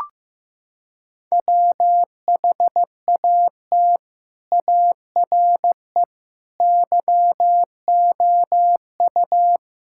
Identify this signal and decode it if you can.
Morse code, heard as a keyed tone, about 15 wpm, 700 Hz. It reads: WHAT ARE YOU
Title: Sentences from Top 1000 Words - Review Part 6 - 15wpm